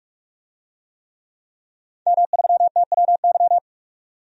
Morse code, heard as a keyed tone, about 46 wpm, 700 Hz. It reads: M3TWY
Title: DXCC-100 Call Signs - 5 Characters - 45wpm